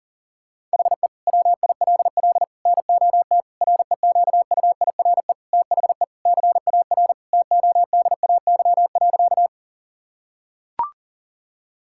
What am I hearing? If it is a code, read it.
Morse code, heard as a keyed tone, about 40 wpm, 700 Hz. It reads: HE WILL NOT REQUIRE THE CAR TODAY.
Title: Sentences from Top 1000 Words - Review Part 6 - 40wpm